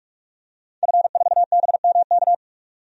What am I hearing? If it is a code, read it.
Morse code, heard as a keyed tone, about 41 wpm, 700 Hz. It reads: F4BMK
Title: DXCC-100 Call Signs - 5 Characters - 45wpm